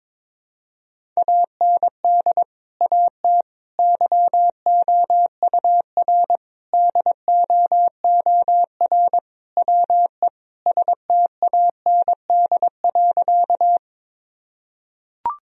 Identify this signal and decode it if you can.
Morse code, heard as a keyed tone, about 22 wpm, 700 Hz. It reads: AND AT YOUR DOOR WE STAND.
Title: Sentences from Top 1000 Words - Review Part 1 - 22wpm